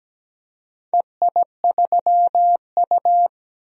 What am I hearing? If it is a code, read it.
Morse code, heard as a keyed tone, about 17 wpm, 700 Hz. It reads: EI3U
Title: DXCC-100 Call Signs - 4 Characters - 17wpm